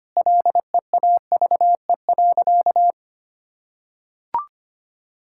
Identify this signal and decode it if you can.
Morse code, heard as a keyed tone, about 25 wpm, 700 Hz. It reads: LEAVE.
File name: practice / 25wpm / Sentences with 4-Word Phrases - 25wpm